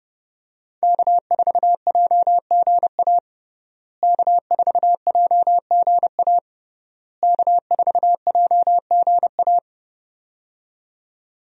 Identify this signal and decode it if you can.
Morse code, heard as a keyed tone, about 32 wpm, 700 Hz. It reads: K4JGA K4JGA K4JGA
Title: USA Call Signs - 5 Characters - Repeated 3x - 30wpm